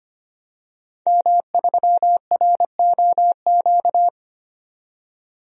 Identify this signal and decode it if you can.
Morse code, heard as a keyed tone, about 25 wpm, 700 Hz. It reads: M3ROQ
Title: DXCC-100 Call Signs - 5 Characters - 25wpm